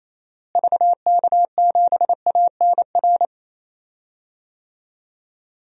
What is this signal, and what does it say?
Morse code, heard as a keyed tone, about 28 wpm, 700 Hz. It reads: VK7ANR
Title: DXCC-100 Call Signs - 6 Characters - Repeated 2x - 28wpm